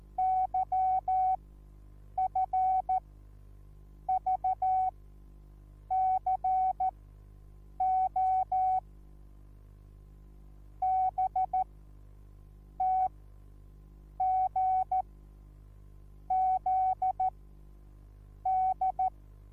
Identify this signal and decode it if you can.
Morse code, heard as a keyed tone, about 14 wpm, spaced as for 6 wpm, 750 Hz. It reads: YFVCO BTGZD
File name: Onlinelehrgang / morsen / DJ4UF Morsen21